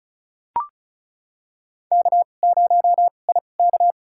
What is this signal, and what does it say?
Morse code, heard as a keyed tone, about 34 wpm, spaced as for 25 wpm, 700 Hz. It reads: K0IK
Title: USA Call Signs - 2x Character Spacing - 35wpm